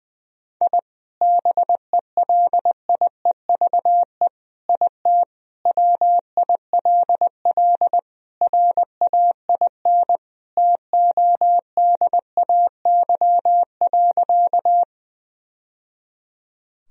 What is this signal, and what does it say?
Morse code, heard as a keyed tone, about 20 wpm, 700 Hz. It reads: I BELIEVE IT WILL RAIN TODAY.
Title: Sentences from Top 1000 Words - Review Part 1 - 20wpm